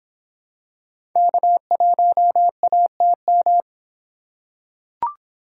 Morse code, heard as a keyed tone, about 25 wpm, 700 Hz. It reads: K1ATM